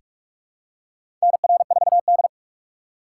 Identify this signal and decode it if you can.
Morse code, heard as a keyed tone, about 44 wpm, 700 Hz. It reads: NR4D
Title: USA Call Signs - 2x Character Spacing - 45wpm